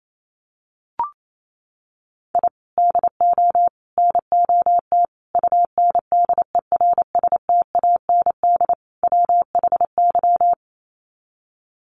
Morse code, heard as a keyed tone, about 28 wpm, 700 Hz. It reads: I DO NOT UNDERSTAND WHY
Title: Sets of 5 Words Encore 28wpm